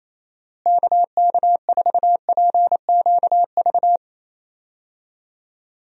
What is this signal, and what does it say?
Morse code, heard as a keyed tone, about 28 wpm, 700 Hz. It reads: KK4PQV
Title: USA Call Signs - 6 Characters - Repeated 2x - 28wpm